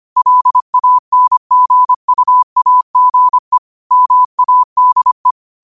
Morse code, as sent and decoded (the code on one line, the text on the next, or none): .-.. .- -. --. ..- .- --. / -- .- -.. .
LANGUAGE MADE